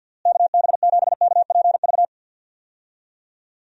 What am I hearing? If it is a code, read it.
Morse code, heard as a keyed tone, about 51 wpm, 700 Hz. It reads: KB7KPV
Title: USA Call Signs - 6 Characters - Repeated 3x - 50wpm